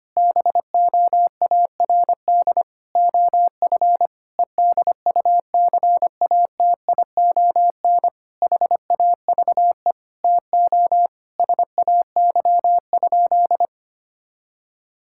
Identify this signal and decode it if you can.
Morse code, heard as a keyed tone, about 25 wpm, 700 Hz. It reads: BOARD OF EDUCATION HAVE TO SAY?
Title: Sentences from Top 1000 Words - Review Part 6 - 25wpm